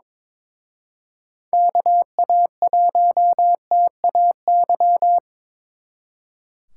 Morse code, heard as a keyed tone, about 26 wpm, 700 Hz. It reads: KA1TAY
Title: Speed Racing - USA 2x3 Call Signs 22wpm